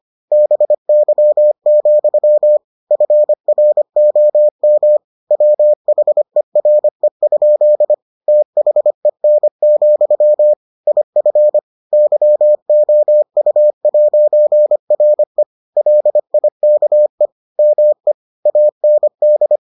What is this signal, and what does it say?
Morse code, heard as a keyed tone, about 25 wpm, 600 Hz. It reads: BY, FROM WHERE? THEN, IF YOU'RE LIKE ME AND